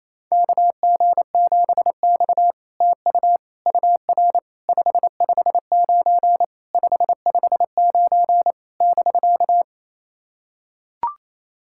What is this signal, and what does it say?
Morse code, heard as a keyed tone, about 28 wpm, 700 Hz. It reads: KG7X TU UR 559 559 <BK>
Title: SOTA QSO Practice 28wpm